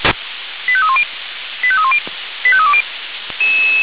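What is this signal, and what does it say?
Five-tone selective call (ZVEI), heard as quick runs of five tones, tones of about 70 ms. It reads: 85310 85310 85320
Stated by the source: ZVEI five-tone selective call received via SDR, from the Feuerwehr Orlamünde audio archive.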